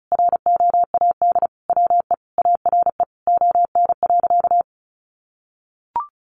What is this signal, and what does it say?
Morse code, heard as a keyed tone, about 34 wpm, 700 Hz. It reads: ROAD WE ARE ON.
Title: Sentences from Top 1100 Words 35wpm